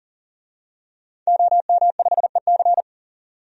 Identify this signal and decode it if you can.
Morse code, heard as a keyed tone, about 40 wpm, 700 Hz. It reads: OM5EC
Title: DXCC-100 Call Signs - 5 Characters - 40wpm